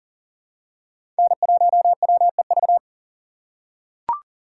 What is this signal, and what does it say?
Morse code, heard as a keyed tone, about 40 wpm, 700 Hz. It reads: N1WEV
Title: USA Call Signs - 5 Characters - 3x Character Spacing - 40wpm